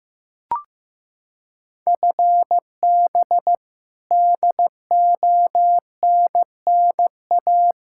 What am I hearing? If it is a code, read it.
Morse code, heard as a keyed tone, about 15 wpm, 700 Hz. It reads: FB DONNA
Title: Practice QSO - Exchange 3 - More Info - 15wpm